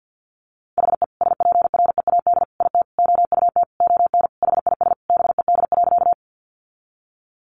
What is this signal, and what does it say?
Morse code, heard as a keyed tone, about 50 wpm, 700 Hz. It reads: HE SPREAD IT OUT ON HIS BED.
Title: Sentences from Top 1200 Words 50wpm